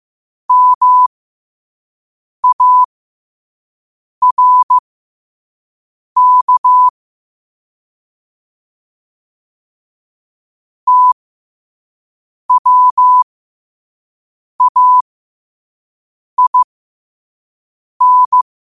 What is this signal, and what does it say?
Morse code, heard as a keyed tone, about 15 wpm, spaced as for 5 wpm, 1000 Hz. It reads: MARK TWAIN
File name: qotdfiles / 5wpm